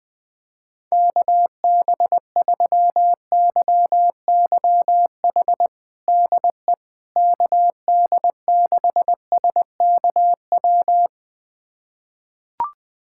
Morse code, heard as a keyed tone, about 20 wpm, 700 Hz. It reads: KB3YYH DE KD6SKW
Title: USA Callsign DE Callsign - 6 Characters - 20wpm